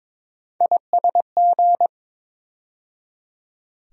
Morse code, heard as a keyed tone, about 22 wpm, 700 Hz. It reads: ISG
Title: USA Call Sign Postfixes - 3 Letters 22wpm